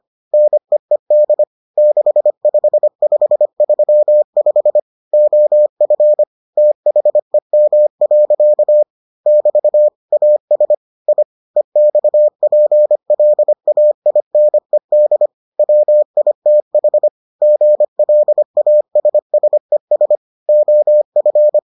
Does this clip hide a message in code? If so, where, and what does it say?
Yes: Morse code from 0.3 to 21.6 s, heard as a keyed tone, about 25 wpm, 600 Hz. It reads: NEED 65535 OF THEM. <BT> AS I EXPLAINED WITH GLASSES OF